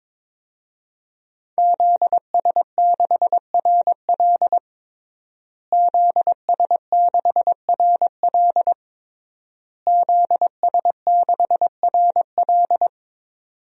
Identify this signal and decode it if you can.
Morse code, heard as a keyed tone, about 22 wpm, 700 Hz. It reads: ZS6RL ZS6RL ZS6RL
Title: DXCC-100 Call Signs - 5 Characters - Repeated 3x - 22wpm